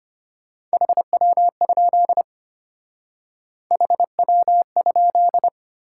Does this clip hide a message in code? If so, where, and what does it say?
Yes: Morse code from 0.7 to 5.5 s, heard as a keyed tone, about 31 wpm, 700 Hz. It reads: HW? HW?